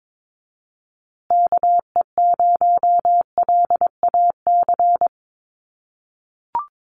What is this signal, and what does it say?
Morse code, heard as a keyed tone, about 22 wpm, 700 Hz. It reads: KE0LAC
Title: USA Call Signs Repeated 3x 22wpm